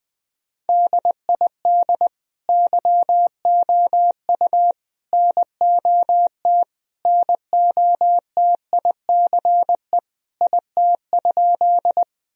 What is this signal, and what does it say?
Morse code, heard as a keyed tone, about 20 wpm, 700 Hz. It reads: DID YOU NOT NOTICE IT?
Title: Sentences from Top 1000 Words - Review Part 2 - 20wpm